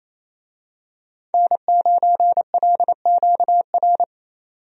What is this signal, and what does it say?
Morse code, heard as a keyed tone, about 28 wpm, 700 Hz. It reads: N9LQR
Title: USA Call Signs - 5 Characters - 28wpm